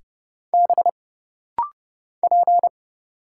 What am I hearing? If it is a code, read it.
Morse code, heard as a keyed tone, about 30 wpm, 700 Hz. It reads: B P
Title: Single Letter-Number Alt - Rapid-Fire 30wpm